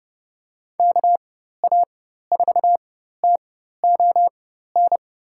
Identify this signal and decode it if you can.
Morse code, heard as a keyed tone, about 30 wpm, spaced as for 14 wpm, 700 Hz. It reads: KA4TON
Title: USA Call Signs - 6 Characters - 4x Character Spacing - 30wpm